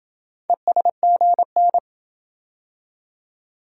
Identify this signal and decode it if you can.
Morse code, heard as a keyed tone, about 27 wpm, 700 Hz. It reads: ESGN